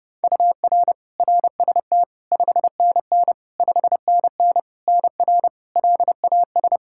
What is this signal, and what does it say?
Morse code, heard as a keyed tone, about 30 wpm, 700 Hz. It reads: UR RST 5NN 5NN NR LAS
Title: K1USN SST QSO Practice 30wpm